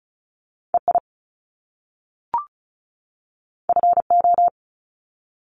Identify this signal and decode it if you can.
Morse code, heard as a keyed tone, about 36 wpm, 700 Hz. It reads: EI FO
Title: Top Two Letter Combinations in Words 35wpm